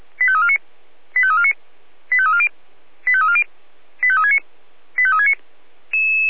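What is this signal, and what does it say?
Five-tone selective call (ZVEI), heard as quick runs of five tones, tones of about 80 ms. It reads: 85349 85349 85340 85340 85369 85369
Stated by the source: ZVEI five-tone call received on an SDR.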